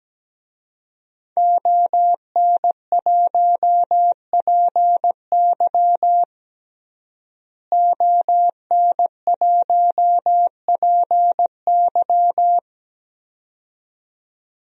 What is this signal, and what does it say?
Morse code, heard as a keyed tone, about 17 wpm, 700 Hz. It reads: ON1PY ON1PY
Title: DXCC-100 Call Signs - 5 Characters - Repeated 2x - 17wpm